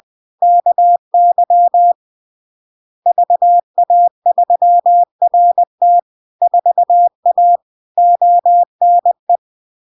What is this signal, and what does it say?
Morse code, heard as a keyed tone, about 20 wpm, 700 Hz. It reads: KY VA3RT 4A ONE
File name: LARK-FD-2022-Log-in-CW-20wpm-700-Hz